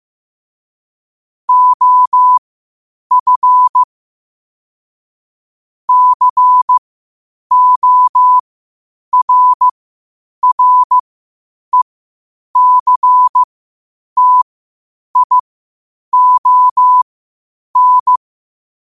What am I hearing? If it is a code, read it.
Morse code, heard as a keyed tone, about 15 wpm, spaced as for 8 wpm, 1000 Hz. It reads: OF CORRECTION
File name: qotdfiles / 8wpm